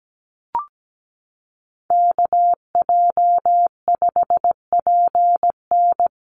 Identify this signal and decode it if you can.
Morse code, heard as a keyed tone, about 17 wpm, 700 Hz. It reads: KJ5PN
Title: USA Call Signs Repeated 2x 17wpm